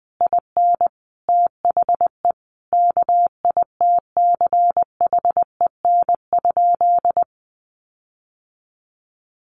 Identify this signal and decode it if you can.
Morse code, heard as a keyed tone, about 20 wpm, 700 Hz. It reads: IN THE KITCHEN?
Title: Sentences from Top 1400 Words 20wpm